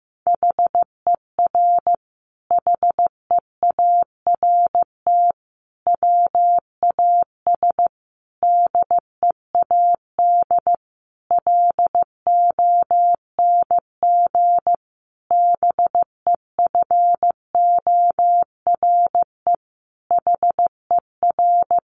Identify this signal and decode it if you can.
Morse code, heard as a keyed tone, about 15 wpm, 700 Hz. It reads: HER HEART WAS DEAD LONG BEFORE HER
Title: Sentences from Top 1500 Words 15wpm